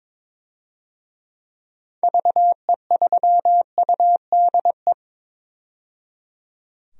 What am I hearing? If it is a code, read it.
Morse code, heard as a keyed tone, about 22 wpm, 700 Hz. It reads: VE3UDE
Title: DXCC-100 Call Signs - 6 Characters - 22wpm